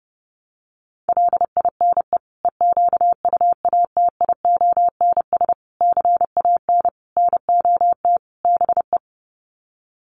Morse code, heard as a keyed tone, about 30 wpm, 700 Hz. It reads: LINE EQUATIONS CAN NOT BE
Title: Sets of 5 Words 30wpm